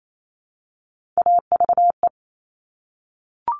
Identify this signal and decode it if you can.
Morse code, heard as a keyed tone, about 28 wpm, 700 Hz. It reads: AVE
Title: Top Three Letter Combinations in Words 28wpm